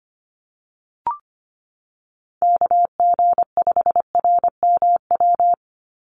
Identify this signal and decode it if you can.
Morse code, heard as a keyed tone, about 25 wpm, 700 Hz. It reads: KG5RMW
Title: USA Call Signs 25wpm